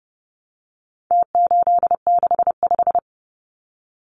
Morse code, heard as a keyed tone, about 30 wpm, 700 Hz. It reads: T865
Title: Numbers - Contest Style - 4-Digit 30wpm